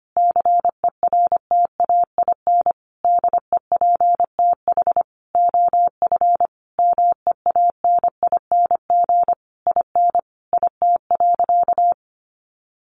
Morse code, heard as a keyed tone, about 25 wpm, 700 Hz. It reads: CERTAIN DEPTH OF MEANING IN IT.
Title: Sentences from Top 1200 Words 25wpm